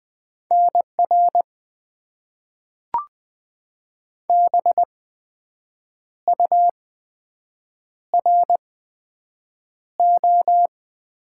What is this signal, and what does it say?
Morse code, heard as a keyed tone, about 20 wpm, spaced as for 7 wpm, 700 Hz. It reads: NR BURO
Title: Top 90 QSO Elements - 8x Character Spacing - 20wpm